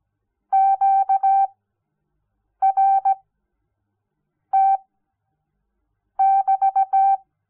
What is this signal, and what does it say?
Morse code, heard as a keyed tone, about 17 wpm, spaced as for 6 wpm, 780 Hz. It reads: QRT <BT>
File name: Onlinelehrgang / morsen / DJ4UF Morsen06